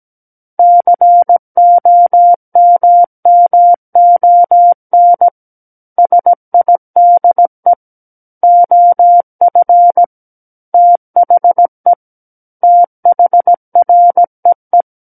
Morse code, heard as a keyed tone, about 17 wpm, 700 Hz. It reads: COMMON SIDE OF THE THREE
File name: Book One / 08-Chapter 8 - 17 WPM